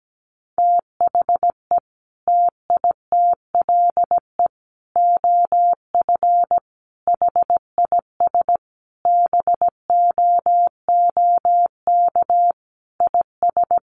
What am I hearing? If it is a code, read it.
Morse code, heard as a keyed tone, about 17 wpm, 700 Hz. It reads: THE TITLE OF HIS BOOK IS
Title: Sentences from Top 1200 Words 17wpm